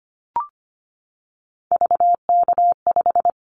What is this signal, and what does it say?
Morse code, heard as a keyed tone, about 25 wpm, 700 Hz. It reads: VK5
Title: Callsign Prefixes for Top 100 DXCC - 25wpm